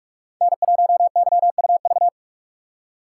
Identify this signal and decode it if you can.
Morse code, heard as a keyed tone, about 44 wpm, 700 Hz. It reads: N1YUV
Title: USA Call Signs - 4x Character Spacing - 45wpm